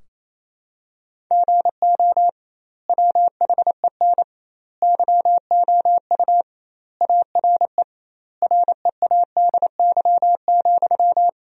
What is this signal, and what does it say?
Morse code, heard as a keyed tone, about 28 wpm, 700 Hz. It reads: GO WHEN YOU ARE READY,
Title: 2x Word-Spacing - Sentences from Top 500 Words - 28wpm